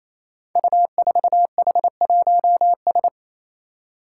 Morse code, heard as a keyed tone, about 28 wpm, 700 Hz. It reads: U4H1S
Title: Five Challenging Letter-Number Combinations 28wpm